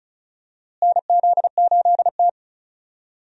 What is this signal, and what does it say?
Morse code, heard as a keyed tone, about 34 wpm, 700 Hz. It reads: NZ8T
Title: USA Call Signs - 4 Characters - Repeated 3x - 35wpm